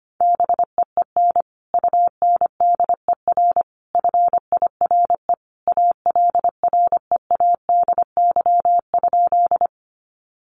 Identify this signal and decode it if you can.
Morse code, heard as a keyed tone, about 25 wpm, 700 Hz. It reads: BEEN UNDER FIRE ALREADY?
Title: Sentences from Top 700 Words 25wpm